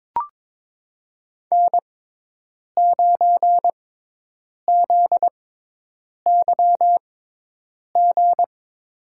Morse code, heard as a keyed tone, about 21 wpm, spaced as for 8 wpm, 700 Hz. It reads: N9ZYG
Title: USA Call Signs - 6x Character Spacing - 22wpm